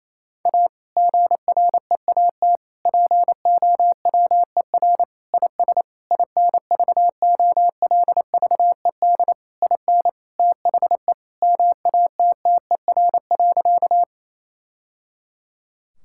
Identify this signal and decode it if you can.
Morse code, heard as a keyed tone, about 28 wpm, 700 Hz. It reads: A GREAT POWER IS INVOLVED IN THE MATTER.
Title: Sentences from Top 1000 Words - Review Part 5 - 28wpm